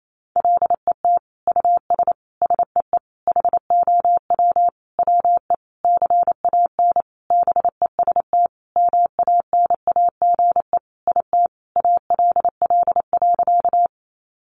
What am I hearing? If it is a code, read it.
Morse code, heard as a keyed tone, about 28 wpm, 700 Hz. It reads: LET US SEE HOW WE CAN BEST MANAGE IT ALL.
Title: Sentences from Top 1400 Words 28wpm